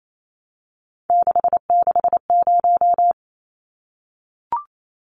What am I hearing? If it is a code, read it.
Morse code, heard as a keyed tone, about 28 wpm, 700 Hz. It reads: 660
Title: Numbers - 3-Digit 28wpm